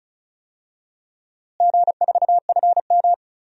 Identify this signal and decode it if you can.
Morse code, heard as a keyed tone, about 36 wpm, 700 Hz. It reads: G4FM
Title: DXCC-100 Call Signs - 4 Characters - Repeated 2x - 35wpm